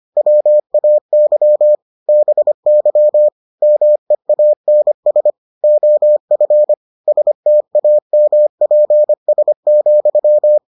Morse code, heard as a keyed tone, about 25 wpm, 600 Hz. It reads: WAY BY MEANS OF STAMPS,